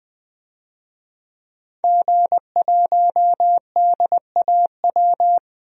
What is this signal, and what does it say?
Morse code, heard as a keyed tone, about 20 wpm, 700 Hz. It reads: G1DAW
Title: DXCC-100 Call Signs - 5 Characters - 20wpm